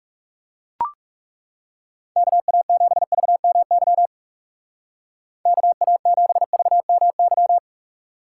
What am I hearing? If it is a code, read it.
Morse code, heard as a keyed tone, about 39 wpm, 700 Hz. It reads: KA7VMY KA7VMY